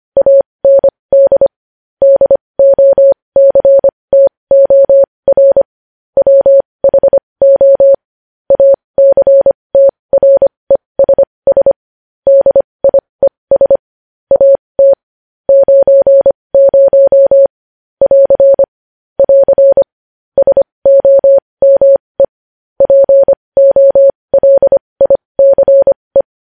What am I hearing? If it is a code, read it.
Morse code, heard as a keyed tone, about 25 wpm, spaced as for 20 wpm, 550 Hz. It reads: AND DOCTOR WHO ACTRESS DIES AT 90 <AR> <AR> SOME POLICE
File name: podcast / News headlines-1x-25x20